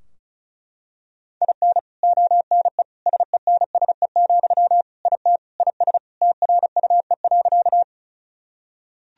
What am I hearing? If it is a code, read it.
Morse code, heard as a keyed tone, about 34 wpm, 700 Hz. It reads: IN ONE SENSE, IT IS TRUE.
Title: Sentences from Top 1000 Words - Review Part 4 - 35wpm